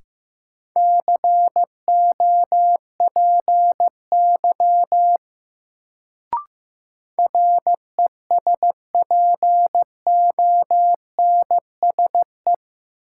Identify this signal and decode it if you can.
Morse code, heard as a keyed tone, about 15 wpm, 700 Hz. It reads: COPY RESPONSE
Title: Sentences from Top 1400 Words - New Word Drill - 15wpm